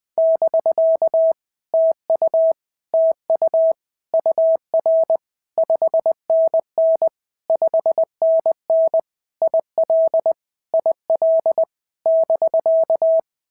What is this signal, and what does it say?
Morse code, heard as a keyed tone, about 20 wpm, 650 Hz. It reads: <BK> TU TU UR 5NN 5NN IL IL <BK>
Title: POTA QSO Practice - 20wpm